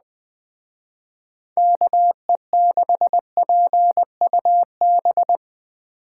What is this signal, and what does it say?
Morse code, heard as a keyed tone, about 20 wpm, 700 Hz. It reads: KE6PUB
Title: Speed Racing - USA Call Signs 17wpm